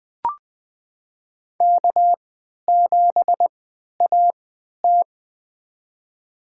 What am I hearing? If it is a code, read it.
Morse code, heard as a keyed tone, about 20 wpm, spaced as for 11 wpm, 700 Hz. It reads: K7AT
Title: USA Call Signs - 4 Characters - 3x Character Spacing - 20wpm